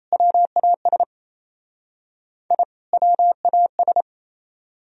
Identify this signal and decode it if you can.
Morse code, heard as a keyed tone, about 35 wpm, 700 Hz. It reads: WAS I WAS